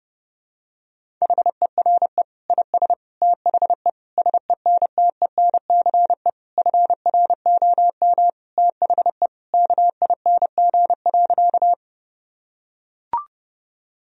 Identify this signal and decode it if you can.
Morse code, heard as a keyed tone, about 32 wpm, 700 Hz. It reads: HERE IS THE SENTENCE FROM THE KING.
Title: Sentences from Top 1000 Words - Review Part 1 - 30wpm